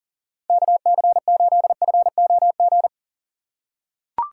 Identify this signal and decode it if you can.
Morse code, heard as a keyed tone, about 40 wpm, 700 Hz. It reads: KC8FOG
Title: USA Call Signs - 6 Characters - 2x Character Spacing - 40wpm